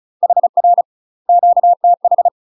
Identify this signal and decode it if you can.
Morse code, heard as a keyed tone, about 34 wpm, 700 Hz. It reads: HR QTH
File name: UK MASTER QSOs TRAINING / 35WPM / UK MASTER QSOs 0034